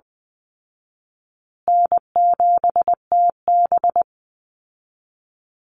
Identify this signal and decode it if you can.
Morse code, heard as a keyed tone, about 20 wpm, 700 Hz. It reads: N7TB
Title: Call Signs Repeated 2x - 20wpm